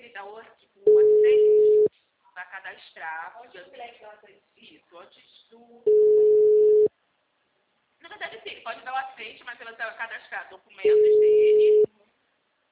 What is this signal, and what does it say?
Ringback tone (Europe, 425 Hz), with speech.